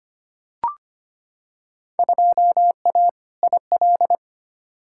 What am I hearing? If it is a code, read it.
Morse code, heard as a keyed tone, about 25 wpm, 700 Hz. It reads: 2A IL
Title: Field Day Exchanges - 25wpm